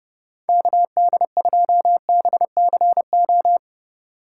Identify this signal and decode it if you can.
Morse code, heard as a keyed tone, about 30 wpm, 700 Hz. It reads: KD2BCO